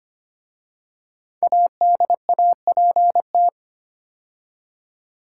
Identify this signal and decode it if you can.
Morse code, heard as a keyed tone, about 25 wpm, 700 Hz. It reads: ADAPT
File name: 5 Letter Words - First 3 Letters Given - 25wpm